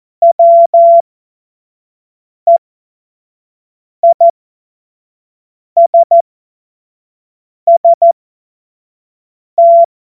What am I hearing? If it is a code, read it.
Morse code, heard as a keyed tone, about 14 wpm, spaced as for 5 wpm, 680 Hz. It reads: WEISST
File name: Morsekurs / Witze / Witz 21. September 2021, 25 F70